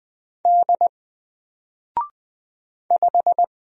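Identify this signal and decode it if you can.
Morse code, heard as a keyed tone, about 20 wpm, 700 Hz. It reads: D 5
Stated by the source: Single letter and number Morse drill at 20 wpm, from N4LCD.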